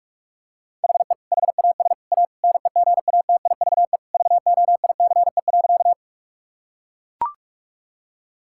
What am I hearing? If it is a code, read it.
Morse code, heard as a keyed tone, about 41 wpm, 700 Hz. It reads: HE HAS A NEGATIVE VOICE.
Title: Sentences from Top 1000 Words - Review Part 6 - 45wpm